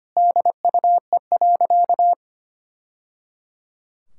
Morse code, heard as a keyed tone, about 25 wpm, 700 Hz. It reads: DUE.
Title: Sentences from Top 1000 Words - Review Part 6 - 25wpm